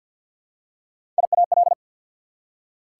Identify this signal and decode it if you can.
Morse code, heard as a keyed tone, about 48 wpm, 690 Hz. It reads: IAL
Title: Prefixes and Suffixes 50wpm